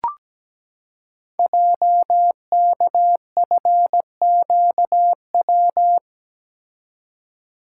Morse code, heard as a keyed tone, about 17 wpm, 700 Hz. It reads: JKFQW
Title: Code Groups - 5 Letters - 17wpm